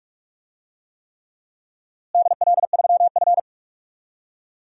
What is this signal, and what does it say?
Morse code, heard as a keyed tone, about 42 wpm, 700 Hz. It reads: DL3F
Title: DXCC-100 Call Signs - 4 Characters - Repeated 3x - 45wpm